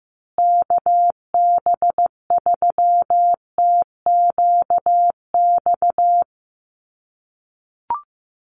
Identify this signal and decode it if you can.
Morse code, heard as a keyed tone, about 15 wpm, 700 Hz. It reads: KB3TQX
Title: USA Call Signs Repeated 2x 15wpm